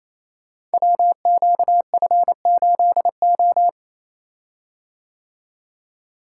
Morse code, heard as a keyed tone, about 27 wpm, 700 Hz. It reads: WQF8O
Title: Code Groups - 5 Letter:Numbers - 28wpm